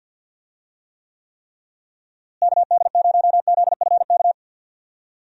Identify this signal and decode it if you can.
Morse code, heard as a keyed tone, about 49 wpm, 700 Hz. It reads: KD0ZFK
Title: USA Call Signs - 6 Characters - Repeated 3x - 50wpm